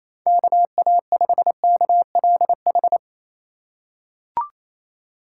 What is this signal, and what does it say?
Morse code, heard as a keyed tone, about 28 wpm, 700 Hz. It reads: KA5KLH
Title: USA Call Signs - 8x Character Spacing - 28wpm